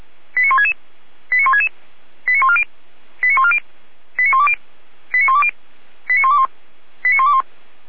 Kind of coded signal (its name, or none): five-tone selective call